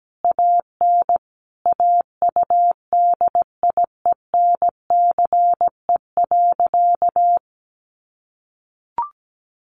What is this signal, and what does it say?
Morse code, heard as a keyed tone, about 17 wpm, 700 Hz. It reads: AN AUDIENCE.
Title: Sentences from Top 800 Words 17wpm